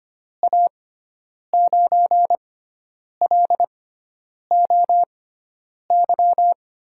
Morse code, heard as a keyed tone, about 25 wpm, spaced as for 9 wpm, 700 Hz. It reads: A9LOY
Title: USA Call Signs - 6 Characters - 6x Character Spacing - 25wpm